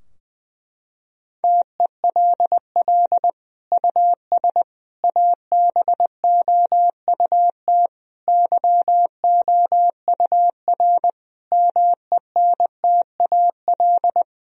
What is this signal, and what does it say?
Morse code, heard as a keyed tone, about 20 wpm, 700 Hz. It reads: TELL US ABOUT YOUR MENTAL
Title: Sentences from Top 1000 Words - Review Part 4 - 20wpm